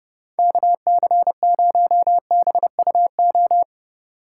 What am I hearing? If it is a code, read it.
Morse code, heard as a keyed tone, about 30 wpm, 700 Hz. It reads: KC0BUO
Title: Speed Racing - USA 2x3 Call Signs 25wpm